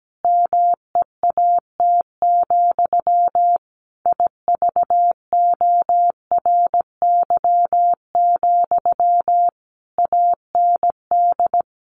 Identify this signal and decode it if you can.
Morse code, heard as a keyed tone, about 17 wpm, 700 Hz. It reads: MEAT, IVORY, AND
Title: A Sound Of Thunder 17wpm